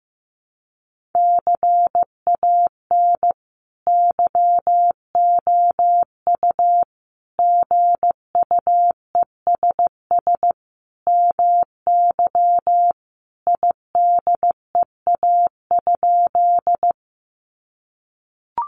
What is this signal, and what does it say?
Morse code, heard as a keyed tone, about 15 wpm, 700 Hz. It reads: CAN YOU GUESS MY IDEA?
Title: Sentences from Top 1500 Words 15wpm